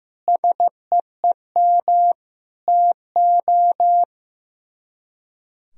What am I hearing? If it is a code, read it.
Morse code, heard as a keyed tone, about 15 wpm, 700 Hz. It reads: SEEM TO
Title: Speed-Racing - Sets of 4 Words 15wpm